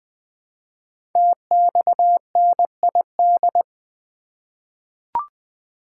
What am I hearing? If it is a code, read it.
Morse code, heard as a keyed tone, about 20 wpm, 700 Hz. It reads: TXNID
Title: Code Groups - 5 Letter:Numbers - 20wpm